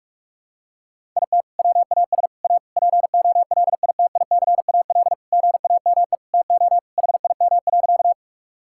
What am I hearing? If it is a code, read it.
Morse code, heard as a keyed tone, about 41 wpm, 700 Hz. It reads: IT WAS A POLITICAL GAME TO HIM.
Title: Sentences from Top 1000 Words - Review Part 5 - 45wpm